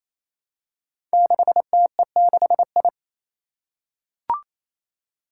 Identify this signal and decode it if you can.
Morse code, heard as a keyed tone, about 27 wpm, 700 Hz. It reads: 6TE6I
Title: Five Challenging Letter-Number Combinations 28wpm